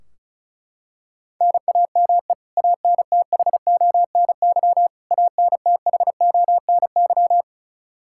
Morse code, heard as a keyed tone, about 36 wpm, 700 Hz. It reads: NAME ANTHONY ANTHONY
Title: Practice QSO - Exchange 1 - Beginning - 35wpm